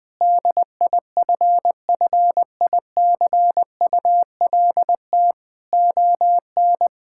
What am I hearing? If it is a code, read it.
Morse code, heard as a keyed tone, about 21 wpm, 700 Hz. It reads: DIFFICULT ON
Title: Sentences from Top 1000 Words - Review Part 5 - 20wpm